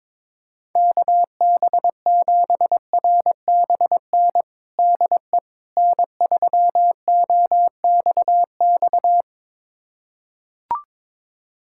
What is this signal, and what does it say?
Morse code, heard as a keyed tone, about 22 wpm, 700 Hz. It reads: KB7RBN DE N3OXX
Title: USA Callsign DE Callsign 22wpm